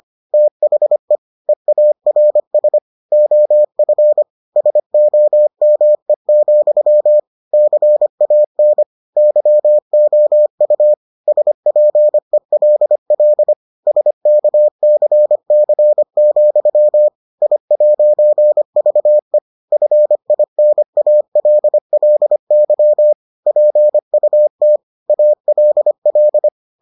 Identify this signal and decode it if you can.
Morse code, heard as a keyed tone, about 25 wpm, 600 Hz. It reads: THE EARS OF SOME, CAN YOU SPELL SKCC, I'VE FINALLY PUT ALL